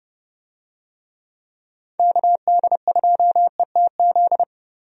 Morse code, heard as a keyed tone, about 30 wpm, 700 Hz. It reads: KD2ETZ
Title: USA Call Signs - 6 Characters - Repeated 3x - 30wpm